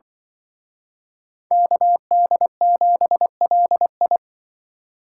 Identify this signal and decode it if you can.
Morse code, heard as a keyed tone, about 25 wpm, 700 Hz. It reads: KD7LI